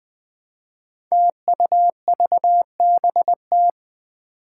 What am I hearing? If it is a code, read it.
Morse code, heard as a keyed tone, about 21 wpm, 700 Hz. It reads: TUVBT
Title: Five Challenging Letter-Number Combinations 20wpm